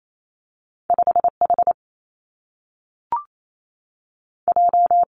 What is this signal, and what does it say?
Morse code, heard as a keyed tone, about 28 wpm, 700 Hz. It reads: 5 H J
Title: Two Challenging Letter-Number Combinations 28wpm